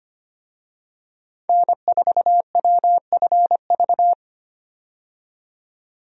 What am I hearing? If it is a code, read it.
Morse code, heard as a keyed tone, about 25 wpm, 700 Hz. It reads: N4WFV
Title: USA Call Signs - 5 Characters - 25wpm